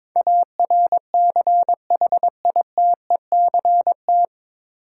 Morse code, heard as a keyed tone, about 22 wpm, 700 Hz. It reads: ARCHITECT
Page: Morse Code Ninja practice 22wpm